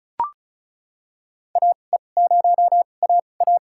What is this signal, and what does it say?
Morse code, heard as a keyed tone, about 36 wpm, spaced as for 25 wpm, 700 Hz. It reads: AE0AA
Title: USA Call Signs - 5 Characters - 2x Character Spacing - 35wpm